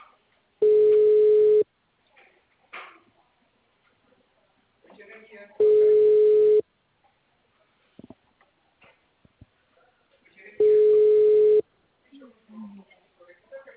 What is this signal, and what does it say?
Ringback tone (Europe, 425 Hz), with speech.
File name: out- -2019- - - .27395